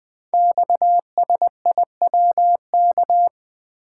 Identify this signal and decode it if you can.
Morse code, heard as a keyed tone, about 20 wpm, 700 Hz. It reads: XSIWK
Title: Code Groups - 5 Letters - 20wpm